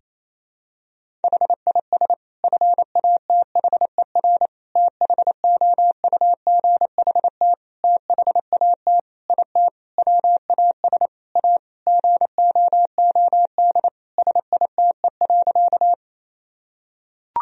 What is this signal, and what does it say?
Morse code, heard as a keyed tone, about 28 wpm, 700 Hz. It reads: HIS FATHER THOUGHT THAT IT WAS A GOOD SITE.
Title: Sentences from Top 1000 Words - Review Part 4 - 28wpm